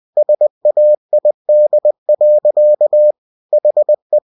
Morse code, heard as a keyed tone, about 20 wpm, 600 Hz. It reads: SAID. HE
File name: cw-books / winnie-the-pooh / winnie-20wpm-page-0089